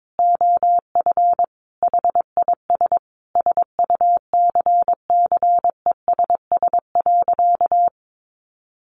Morse code, heard as a keyed tone, about 22 wpm, 700 Hz. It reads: OF HIS SUCCESS.
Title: Sentences from Top 900 Words 22wpm